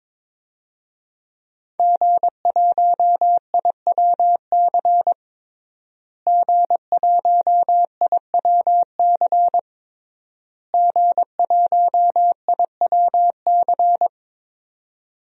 Morse code, heard as a keyed tone, about 22 wpm, 700 Hz. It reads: G1IWC G1IWC G1IWC
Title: DXCC-100 Call Signs - 5 Characters - Repeated 3x - 22wpm